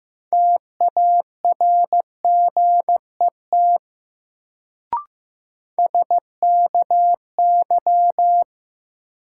Morse code, heard as a keyed tone, about 15 wpm, 700 Hz. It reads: TARGET SKY
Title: Sentences from Top 1200 Words - New Word Drill - 15wpm